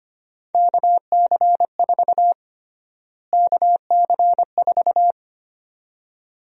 Morse code, heard as a keyed tone, about 25 wpm, 700 Hz. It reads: KC4 KC4
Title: USA Call Sign Prefixes - Repeated 2x - 25wpm